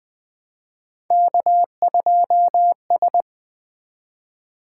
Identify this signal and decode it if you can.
Morse code, heard as a keyed tone, about 20 wpm, 700 Hz. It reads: K2S
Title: Three Letter-Number Combinations - 20wpm